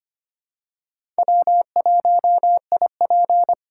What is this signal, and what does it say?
Morse code, heard as a keyed tone, about 25 wpm, 700 Hz. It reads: W1IP
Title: USA Call Signs - 4 Characters - 4x Character Spacing - 25wpm